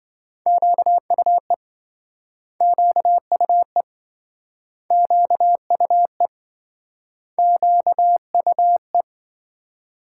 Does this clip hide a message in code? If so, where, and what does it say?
Yes: Morse code from 0.5 to 9.0 s, heard as a keyed tone, about 26 wpm, 700 Hz. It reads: QUE QUE QUE QUE